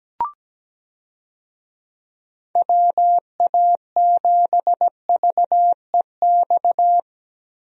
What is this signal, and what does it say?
Morse code, heard as a keyed tone, about 17 wpm, 700 Hz. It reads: WA7VEX
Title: USA Call Signs - 6 Characters - Repeated 2x - 17wpm